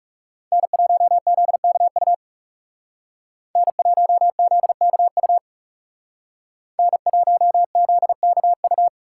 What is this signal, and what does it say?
Morse code, heard as a keyed tone, about 39 wpm, 700 Hz. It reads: N1ZKU N1ZKU N1ZKU